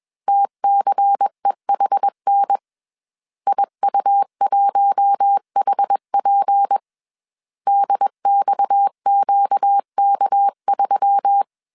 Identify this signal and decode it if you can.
Morse code, heard as a keyed tone, about 21 wpm, 800 Hz. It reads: TCEHN IU1HP DXQK3
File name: CW / DET / 1 6 1500C S A 20wpm 2048